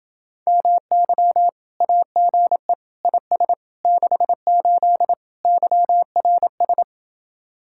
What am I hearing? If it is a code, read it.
Morse code, heard as a keyed tone, about 31 wpm, 700 Hz. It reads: MY AGE IS 68 YRS